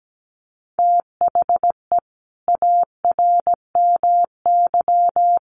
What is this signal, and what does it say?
Morse code, heard as a keyed tone, about 17 wpm, 700 Hz. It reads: THE ARMY
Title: Sentences from Top 1300 Words 17wpm